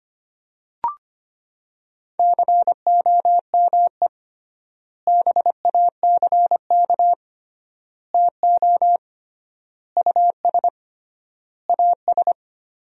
Morse code, heard as a keyed tone, about 25 wpm, 700 Hz. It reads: COME BACK TO US AS
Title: 3x Word-Spacing - Sentences from Top 500 Words - 25wpm